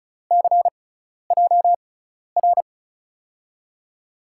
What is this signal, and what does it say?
Morse code, heard as a keyed tone, about 36 wpm, spaced as for 12 wpm, 700 Hz. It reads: CJR
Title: USA Call Signs - 6x Character Spacing - 35wpm